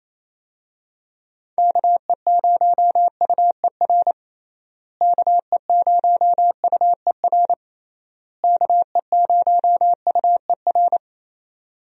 Morse code, heard as a keyed tone, about 27 wpm, 700 Hz. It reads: KE0UER KE0UER KE0UER
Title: USA Call Signs - 6 Characters - Repeated 3x - 28wpm